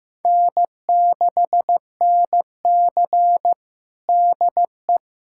Morse code, heard as a keyed tone, about 15 wpm, 700 Hz. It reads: N6NC DE
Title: USA Callsign DE Callsign - 4 Characters - 15wpm